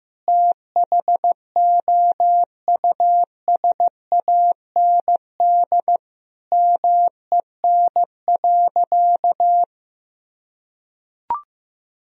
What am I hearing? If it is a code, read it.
Morse code, heard as a keyed tone, about 16 wpm, 700 Hz. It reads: THOUSAND MEN.
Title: Sentences from Top 1000 Words - Review Part 3 - 15wpm